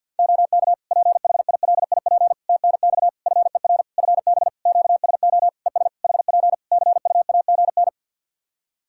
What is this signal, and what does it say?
Morse code, heard as a keyed tone, about 50 wpm, 700 Hz. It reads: OK PHILIP TNX FER FB QSO ES HP CUAGN
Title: Practice QSO - Exchange 4 - Ending - 50wpm